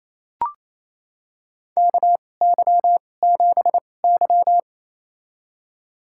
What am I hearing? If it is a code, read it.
Morse code, heard as a keyed tone, about 28 wpm, spaced as for 20 wpm, 700 Hz. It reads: KY7Y
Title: USA Call Signs - 4 Characters - 2x Character Spacing - 28wpm